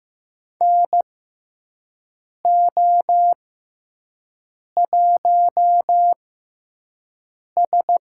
Morse code, heard as a keyed tone, about 15 wpm, spaced as for 5 wpm, 700 Hz. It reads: NO1S
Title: USA Call Signs - 4 Characters - 6x Character Spacing - 15wpm